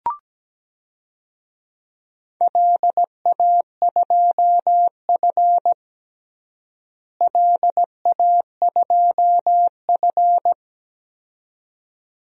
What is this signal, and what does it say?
Morse code, heard as a keyed tone, about 17 wpm, 700 Hz. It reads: LA2F LA2F
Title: DXCC-100 Call Signs - 4 Characters - Repeated 2x - 17wpm